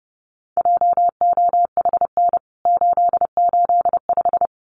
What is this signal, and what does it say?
Morse code, heard as a keyed tone, about 30 wpm, 700 Hz. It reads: JOHN 885
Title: CWT Exchange 30wpm